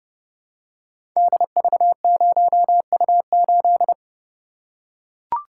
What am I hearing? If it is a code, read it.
Morse code, heard as a keyed tone, about 29 wpm, 700 Hz. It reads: DV0U8
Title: Code Groups - 5 Letter:Numbers - 30wpm